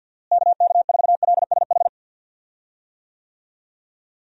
Morse code, heard as a keyed tone, about 52 wpm, 700 Hz. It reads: KK4LSH
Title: USA Call Signs - 6 Characters - Repeated 3x - 50wpm